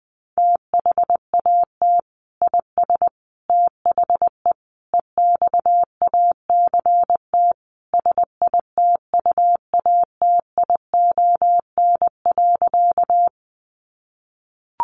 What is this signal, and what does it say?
Morse code, heard as a keyed tone, about 20 wpm, 700 Hz. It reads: THAT IS THE EXACT SITUATION.
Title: Sentences from Top 1300 Words 20wpm